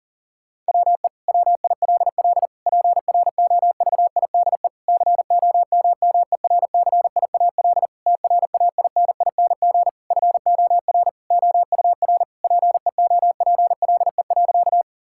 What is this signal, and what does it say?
Morse code, heard as a keyed tone, about 40 wpm, 700 Hz. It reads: WE WILL PROVIDE COMMERCIAL TRAINING FOR OUR PEOPLE.
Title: Sentences from Top 1000 Words - Review Part 5 - 40wpm